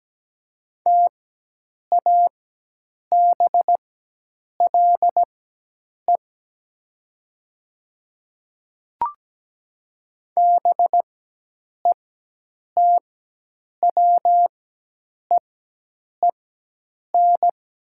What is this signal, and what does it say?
Morse code, heard as a keyed tone, about 17 wpm, spaced as for 8 wpm, 700 Hz. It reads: TABLE BETWEEN
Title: 4x Character Spacing - Top 500 Words - 17wpm